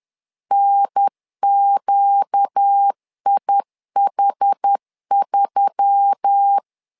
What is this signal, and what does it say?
Morse code, heard as a keyed tone, about 10 wpm, 800 Hz. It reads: NQIH3
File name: LEZIONI / 5 LBVC / 5 LBV 10wpm 2248